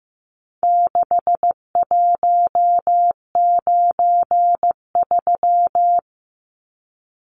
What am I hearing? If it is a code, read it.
Morse code, heard as a keyed tone, about 15 wpm, 700 Hz. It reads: 6193
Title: Numbers - 4-Digit 15wpm